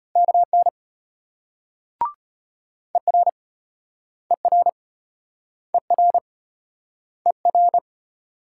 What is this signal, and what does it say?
Morse code, heard as a keyed tone, about 34 wpm, 700 Hz. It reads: KN ER ER ER ER